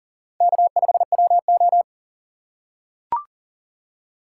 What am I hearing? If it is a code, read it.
Morse code, heard as a keyed tone, about 38 wpm, 700 Hz. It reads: K5WO
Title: USA Call Signs - 4 Characters - 3x Character Spacing - 40wpm